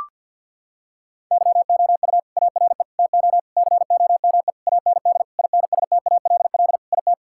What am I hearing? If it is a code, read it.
Morse code, heard as a keyed tone, about 51 wpm, 700 Hz. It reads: YOU ARE TO COME AND INSTALL IT
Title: Sentences from Top 1600 Words 50wpm